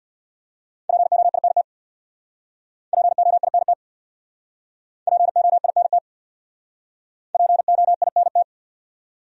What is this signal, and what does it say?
Morse code, heard as a keyed tone, about 62 wpm, 700 Hz. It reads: POINT POINT POINT POINT
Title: Speed-Racing Top 200 Words 50wpm